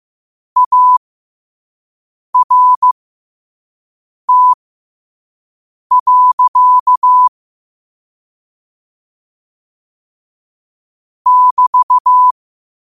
Morse code, heard as a keyed tone, about 15 wpm, spaced as for 5 wpm, 1000 Hz. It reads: ART. <BT>
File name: qotdfiles / 5wpm